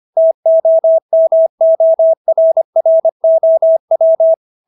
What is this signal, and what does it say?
Morse code, heard as a keyed tone, about 25 wpm, 650 Hz. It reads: TOMORROW